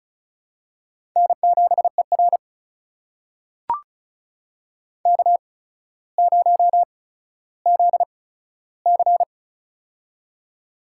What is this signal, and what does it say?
Morse code, heard as a keyed tone, about 36 wpm, spaced as for 15 wpm, 700 Hz. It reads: N7ER K0ZC
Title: USA Call Signs - 4 Characters - 8x Character Spacing - 35wpm